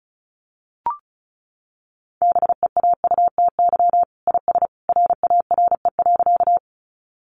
Morse code, heard as a keyed tone, about 34 wpm, 700 Hz. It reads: BEAUTY IS RARE.
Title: Sentences from Top 1100 Words 35wpm